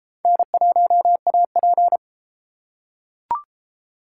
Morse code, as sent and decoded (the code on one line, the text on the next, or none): -. .---- .- .--.
N1AP